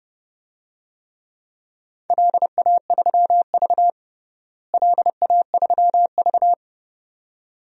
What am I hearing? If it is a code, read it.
Morse code, heard as a keyed tone, about 32 wpm, 700 Hz. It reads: LA3V LA3V
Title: DXCC-100 Call Signs - 4 Characters - Repeated 2x - 30wpm